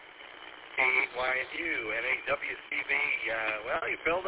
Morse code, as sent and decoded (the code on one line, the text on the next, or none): none